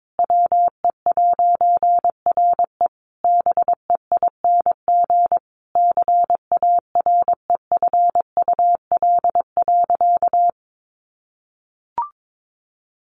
Morse code, heard as a keyed tone, about 22 wpm, 700 Hz. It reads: WE'RE BEING CAREFUL.
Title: A Sound Of Thunder 22wpm